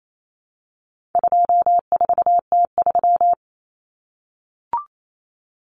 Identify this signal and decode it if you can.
Morse code, heard as a keyed tone, about 28 wpm, 700 Hz. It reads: 24T3
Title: Common Cut Numbers - 4-Digit 28wpm